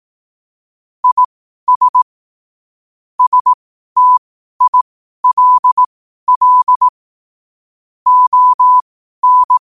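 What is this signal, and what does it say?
Morse code, heard as a keyed tone, about 18 wpm, spaced as for 12 wpm, 1000 Hz. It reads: IS STILL ON
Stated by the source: Morse code quotes at 12 words per minute.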